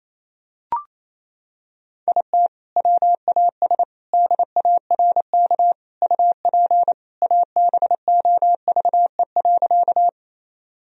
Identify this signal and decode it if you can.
Morse code, heard as a keyed tone, about 28 wpm, 700 Hz. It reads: IT WAS DARK UP ABOVE.
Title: Sentences from Top 1000 Words - Review Part 2 - 28wpm